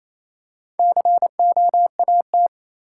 Morse code, heard as a keyed tone, about 28 wpm, 700 Hz. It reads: COAT
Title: Four Letter Words - 28wpm